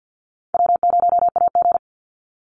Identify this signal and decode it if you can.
Morse code, heard as a keyed tone, about 50 wpm, 700 Hz. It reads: W0AG
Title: Easy Call Signs - 50wpm